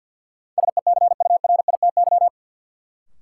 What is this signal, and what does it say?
Morse code, heard as a keyed tone, about 49 wpm, 700 Hz. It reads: SECURITY
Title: Sentences from Top 1400 Words - New Word Drill Alt - 50wpm